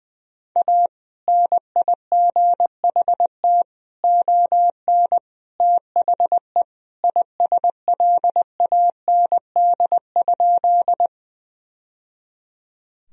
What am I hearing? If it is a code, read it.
Morse code, heard as a keyed tone, about 20 wpm, 700 Hz. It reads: A NIGHT ON THE ISLAND?
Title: Sentences from Top 1000 Words - Review Part 3 - 20wpm